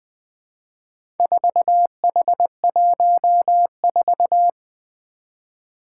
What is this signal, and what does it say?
Morse code, heard as a keyed tone, about 20 wpm, 700 Hz. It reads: 4H14
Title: Four Challenging Letter-Number Combinations 20wpm